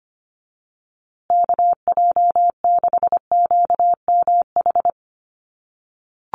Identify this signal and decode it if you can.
Morse code, heard as a keyed tone, about 25 wpm, 700 Hz. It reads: KJ6QMH
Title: USA Call Signs Repeated 2x 25wpm